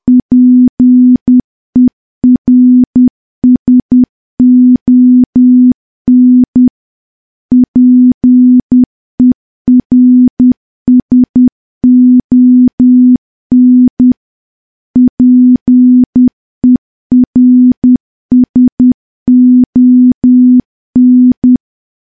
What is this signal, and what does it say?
Morse code, heard as a keyed tone, about 10 wpm, 250 Hz. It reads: PERSON PERSON PERSON
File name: morse / morse-training-10WPM